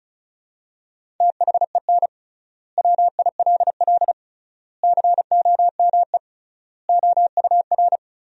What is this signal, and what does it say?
Morse code, heard as a keyed tone, about 36 wpm, 700 Hz. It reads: THEN WILL COME OUR
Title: 3x Word-Spacing - Sentences from Top 200 Words - 35wpm